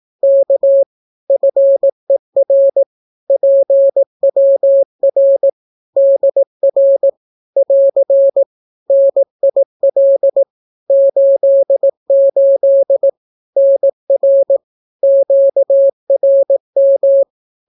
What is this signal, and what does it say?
Morse code, heard as a keyed tone, about 18 wpm, 550 Hz. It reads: K FER PWR DR <AR> NIL 88 NR QRM
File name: naukapodstaw / skroty / skroty